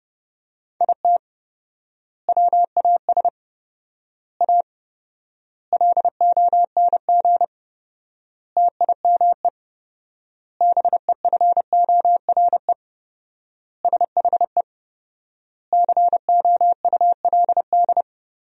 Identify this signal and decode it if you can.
Morse code, heard as a keyed tone, about 30 wpm, 700 Hz. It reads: IT WAS A LONG TIME BEFORE SHE COULD
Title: 4x Word-Spacing - Sentences from Top 500 Words - 30wpm